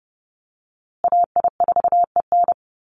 Morse code, heard as a keyed tone, about 30 wpm, 700 Hz. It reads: AI4EN
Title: USA Call Signs Repeated 2x 30wpm